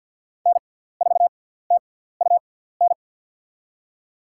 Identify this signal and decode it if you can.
Morse code, heard as a keyed tone, about 50 wpm, spaced as for 17 wpm, 700 Hz. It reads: N4TUN
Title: USA Call Signs - 5 Characters - 6x Character Spacing - 50wpm